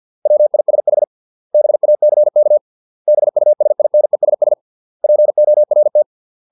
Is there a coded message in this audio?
Yes: Morse code from 0.3 to 6.0 s, heard as a keyed tone, about 50 wpm, 600 Hz. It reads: WISH BACK BUSINESS PORT